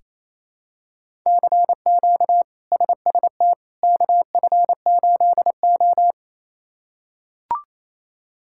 Morse code, heard as a keyed tone, about 28 wpm, 700 Hz. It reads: CQ SST KF8O
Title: K1USN SST QSO Practice - 28wpm